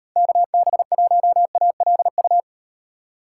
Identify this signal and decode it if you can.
Morse code, heard as a keyed tone, about 36 wpm, 700 Hz. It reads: KB1ALU